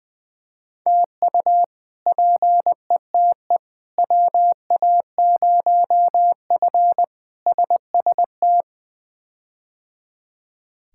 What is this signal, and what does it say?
Morse code, heard as a keyed tone, about 20 wpm, 700 Hz. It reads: TU PETE WA0F SST
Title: K1USN SST QSO Practice - 20wpm